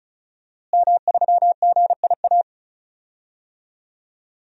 Morse code, heard as a keyed tone, about 34 wpm, 700 Hz. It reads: M3GIA
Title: DXCC-100 Call Signs - 5 Characters - Repeated 2x - 35wpm